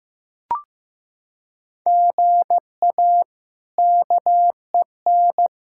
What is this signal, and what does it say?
Morse code, heard as a keyed tone, about 15 wpm, 700 Hz. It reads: GA KEN
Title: K1USN SST QSO Practice - 15wpm